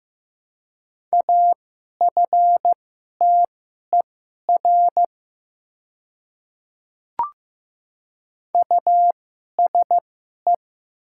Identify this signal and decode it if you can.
Morse code, heard as a keyed tone, about 15 wpm, spaced as for 11 wpm, 700 Hz. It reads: AFTER USE
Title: 2x Character Spacing - Top 500 Words - 15wpm